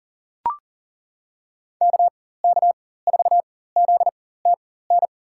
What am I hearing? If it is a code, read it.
Morse code, heard as a keyed tone, about 40 wpm, spaced as for 19 wpm, 700 Hz. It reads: KK4ZTN
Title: USA Call Signs - 4x Character Spacing - 40wpm